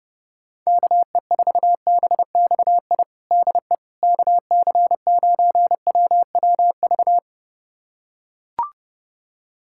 Morse code, heard as a keyed tone, about 32 wpm, 700 Hz. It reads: KE4BXI DE KC9WWV
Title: USA Callsign DE Callsign 30wpm